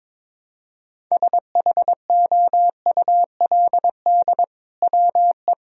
Morse code, heard as a keyed tone, about 21 wpm, 700 Hz. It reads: SHOULD WE
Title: Sentences from Top 1000 Words - Review Part 3 - 22wpm